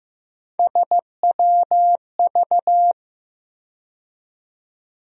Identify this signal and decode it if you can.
Morse code, heard as a keyed tone, about 15 wpm, 700 Hz. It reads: SWV
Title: USA Call Sign Postfixes - 3 Letters 15wpm